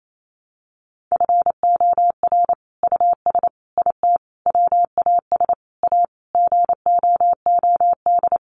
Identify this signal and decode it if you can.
Morse code, heard as a keyed tone, about 28 wpm, 700 Hz. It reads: FOR US IT WAS A GOOD
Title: Sentences from Top 1200 Words 28wpm